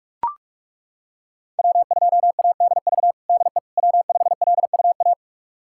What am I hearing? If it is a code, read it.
Morse code, heard as a keyed tone, about 41 wpm, 700 Hz. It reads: W2ADV DE W5LUA
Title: USA Callsign DE Callsign - 5 Characters - 45wpm